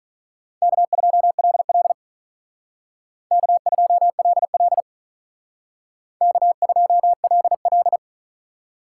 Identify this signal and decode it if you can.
Morse code, heard as a keyed tone, about 45 wpm, 700 Hz. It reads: K2LL K2LL K2LL